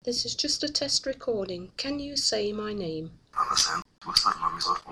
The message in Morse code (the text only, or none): none